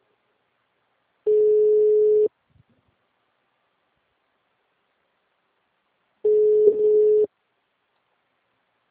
Telephone call progress tone (Europe, 425 Hz), heard as a telephone network's tone, about 420 Hz. Ringback tone.